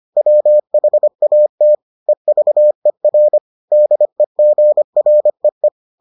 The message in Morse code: WHATEVER DEGREE

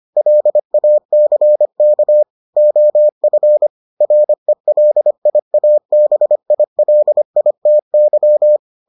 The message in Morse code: LACK OF RELIABILITY

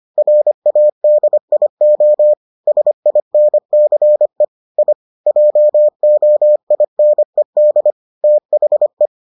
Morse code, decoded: RADIO SINCE I JOINED THE

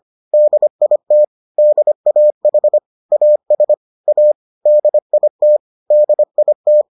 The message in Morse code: DIT DAH AS A DIT DIT

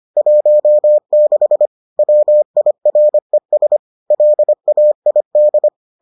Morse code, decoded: 16 WIRES LAID

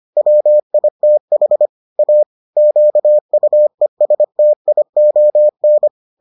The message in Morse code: WITH A QUESTION

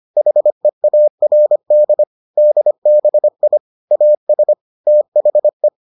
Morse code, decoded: HEARD DBI AS THE

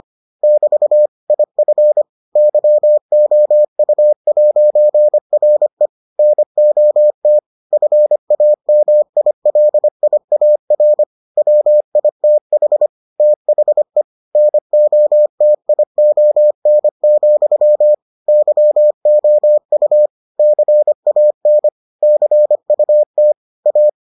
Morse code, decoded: <BT> IF YOU'RE NOT FAMILIAR WITH THE NOTION, YOU CAN CUT A

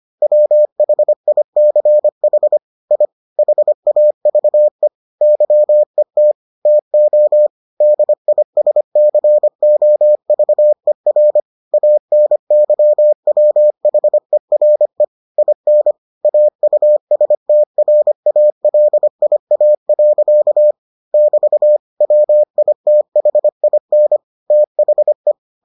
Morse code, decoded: WHICH I HAVE YET TO DISCOVER ANYWHERE IN AUSTRALIA. <BT> WITHIN THE

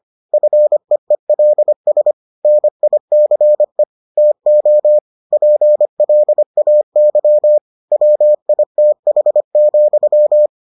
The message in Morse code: FEELS NICE TO PLAY WITH,

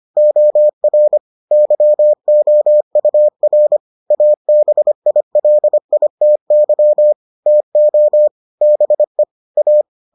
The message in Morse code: OR YOUR ABILITY TO BE A

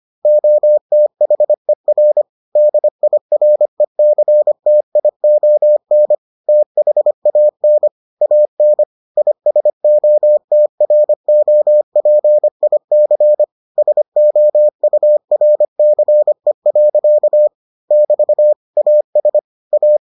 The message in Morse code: OTHER DIRECTION THAN AN ISOTROPIC SOURCE. <BT> AS A

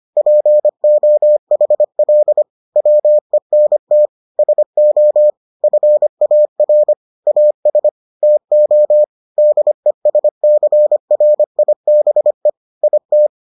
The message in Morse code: POHL WENT SO FAR AS TO DESCRIBE IT

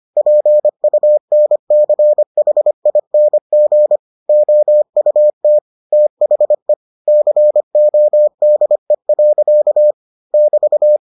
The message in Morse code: PUNCHING OUT THE CODE. <BT>

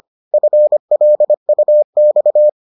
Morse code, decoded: FLUX

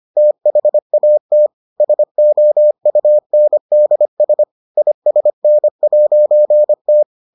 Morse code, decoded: THAT SOUNDS ISN'T

